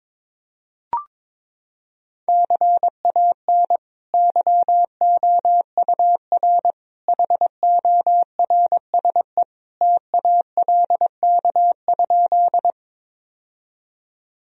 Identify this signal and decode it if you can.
Morse code, heard as a keyed tone, about 21 wpm, 700 Hz. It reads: CAN YOUR HORSE TALK?
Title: Sentences from Top 1000 Words - Review Part 1 - 22wpm